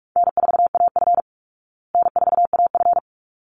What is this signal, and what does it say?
Morse code, heard as a keyed tone, about 44 wpm, 700 Hz. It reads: N4AF N4AF
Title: Easy Call Signs Repeated 2x - 45wpm